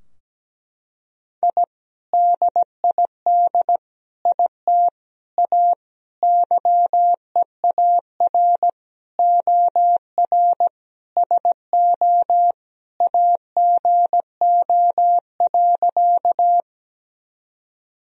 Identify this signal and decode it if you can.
Morse code, heard as a keyed tone, about 17 wpm, 700 Hz. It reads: I DID IT A YEAR OR SO AGO.
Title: Sentences from Top 1000 Words - Review Part 3 - 17wpm